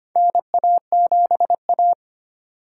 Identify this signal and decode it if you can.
Morse code, heard as a keyed tone, about 25 wpm, 700 Hz. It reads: NA7A
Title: USA Call Signs - 4 Characters - 2x Character Spacing - 25wpm